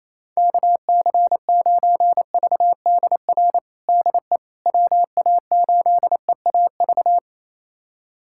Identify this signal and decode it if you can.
Morse code, heard as a keyed tone, about 28 wpm, 700 Hz. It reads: KC9VDR DE WA8EAV
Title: USA Callsign DE Callsign 28wpm